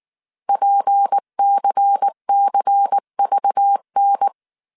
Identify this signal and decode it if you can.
Morse code, heard as a keyed tone, about 19 wpm, 800 Hz. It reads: PCCVN
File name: CW / DET / 1 6 1500C S C 18wpm 2356